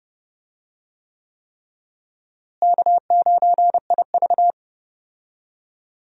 Morse code, heard as a keyed tone, about 31 wpm, 700 Hz. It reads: K9IV
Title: USA Call Signs - 4 Characters - Repeated 3x - 30wpm